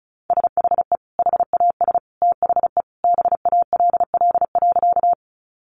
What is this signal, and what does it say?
Morse code, heard as a keyed tone, about 34 wpm, 700 Hz. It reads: SHE HAS THE BALL.
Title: Sentences from Top 900 Words 35wpm